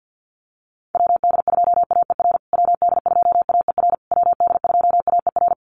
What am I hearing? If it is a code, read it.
Morse code, heard as a keyed tone, about 48 wpm, 700 Hz. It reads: WD2AER WD2AER WD2AER
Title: USA Call Signs Repeated 3x 50wpm